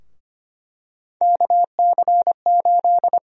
Morse code, heard as a keyed tone, about 25 wpm, 700 Hz. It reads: KC8
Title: USA Call Sign Prefixes - Repeated 2x - 25wpm